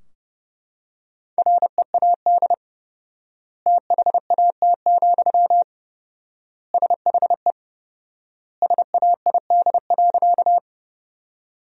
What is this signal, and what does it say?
Morse code, heard as a keyed tone, about 32 wpm, 700 Hz. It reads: READ THAT, SHE SAID.
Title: 4x Word-Spacing - Sentences from Top 200 Words - 30wpm